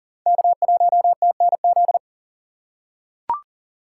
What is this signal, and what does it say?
Morse code, heard as a keyed tone, about 40 wpm, 700 Hz. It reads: K1TNZ
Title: USA Call Signs - 2x Character Spacing - 40wpm